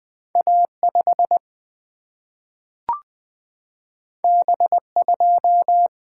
Morse code, heard as a keyed tone, about 20 wpm, 700 Hz. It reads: A 5 B 2
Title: Single-Letter Single-Number 20wpm